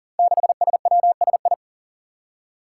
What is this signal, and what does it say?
Morse code, heard as a keyed tone, about 40 wpm, 700 Hz. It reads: 6SWSI
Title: Five Challenging Letter-Number Combinations 40wpm